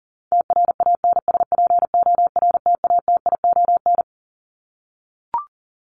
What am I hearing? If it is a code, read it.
Morse code, heard as a keyed tone, about 40 wpm, 700 Hz. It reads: TRANSPORTATION
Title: Top 1501-1750 Words 40wpm